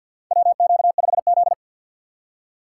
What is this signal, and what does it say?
Morse code, heard as a keyed tone, about 48 wpm, 700 Hz. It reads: WQ5Z
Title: USA Call Signs - 4 Characters - 6x Character Spacing - 50wpm